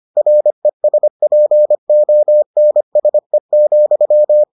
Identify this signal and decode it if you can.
Morse code, heard as a keyed tone, about 25 wpm, 600 Hz. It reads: RESPONSE,